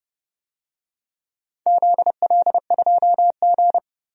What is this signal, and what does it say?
Morse code, heard as a keyed tone, about 30 wpm, 700 Hz. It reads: ZL2G
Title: DXCC-100 Call Signs - 4 Characters - 30wpm